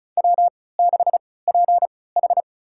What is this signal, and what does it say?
Morse code, heard as a keyed tone, about 34 wpm, spaced as for 20 wpm, 700 Hz. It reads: W6PH
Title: USA Call Signs - 4 Characters - 3x Character Spacing - 35wpm